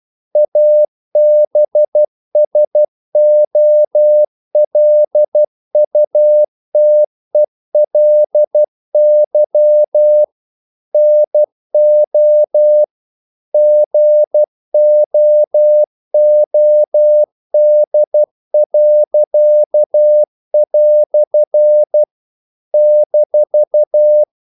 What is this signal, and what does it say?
Morse code, heard as a keyed tone, about 12 wpm, 600 Hz. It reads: ABSOLUTELY NO GOOD." -
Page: English quotations in CW (telegraphy - morse)